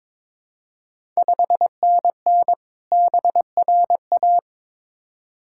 Morse code, heard as a keyed tone, about 22 wpm, 700 Hz. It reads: 5NN BRA
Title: Ontario QSO Party Exchange 22wpm